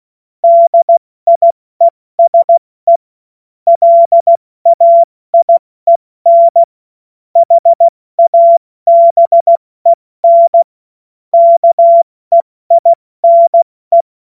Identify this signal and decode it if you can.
Morse code, heard as a keyed tone, about 16 wpm, 680 Hz. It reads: DIESE LAIEN HABEN KEINE
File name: Morsekurs / Witze / Witz 10.Oktober 2023 70 F80